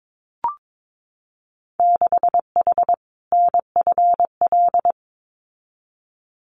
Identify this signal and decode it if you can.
Morse code, heard as a keyed tone, about 22 wpm, 700 Hz. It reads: 6H NFL
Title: Winter Field Day Exchanges Encore 22wpm